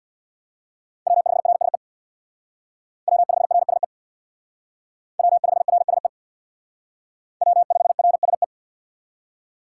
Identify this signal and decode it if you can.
Morse code, heard as a keyed tone, about 65 wpm, 700 Hz. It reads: W5RSE W5RSE W5RSE W5RSE